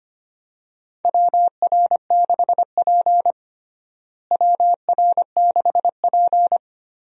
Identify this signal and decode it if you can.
Morse code, heard as a keyed tone, about 25 wpm, 700 Hz. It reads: WR6P WR6P
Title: USA Call Signs - 4 Characters - Repeated 2x - 25wpm